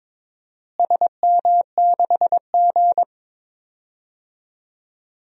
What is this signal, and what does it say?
Morse code, heard as a keyed tone, about 22 wpm, 700 Hz. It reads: SM6G
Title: DXCC-100 Call Signs - 4 Characters - Repeated 3x - 22wpm